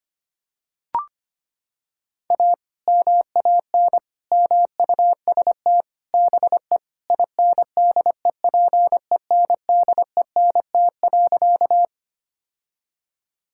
Morse code, heard as a keyed tone, about 25 wpm, 700 Hz. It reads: A MAN MUST BE INDEPENDENT.
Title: Sentences from Top 1000 Words - Review Part 6 - 25wpm